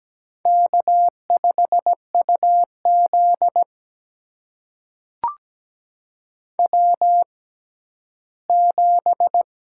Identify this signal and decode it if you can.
Morse code, heard as a keyed tone, about 17 wpm, spaced as for 11 wpm, 700 Hz. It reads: K5UZ W7
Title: USA Call Signs - 4 Characters - 6x Character Spacing - 17wpm